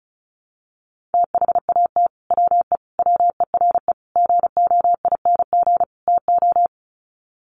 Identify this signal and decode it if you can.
Morse code, heard as a keyed tone, about 34 wpm, 700 Hz. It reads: THAT WE WERE GOING TO
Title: Sets of 5 Words Encore2 35wpm